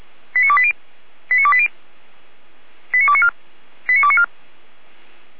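Five-tone selective call (ZVEI), heard as quick runs of five tones, tones of about 70 ms. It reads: 89280 89280 89294 89294
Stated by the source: ZVEI five-tone call received by SDR.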